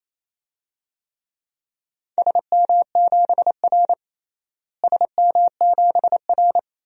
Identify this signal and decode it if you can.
Morse code, heard as a keyed tone, about 28 wpm, 700 Hz. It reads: SM7R SM7R
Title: DXCC-100 Call Signs - 4 Characters - Repeated 2x - 28wpm